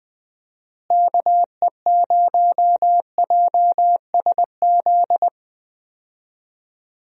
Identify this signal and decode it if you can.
Morse code, heard as a keyed tone, about 20 wpm, 700 Hz. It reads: KE0JSZ
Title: USA Call Signs - 6 Characters - Repeated 2x - 20wpm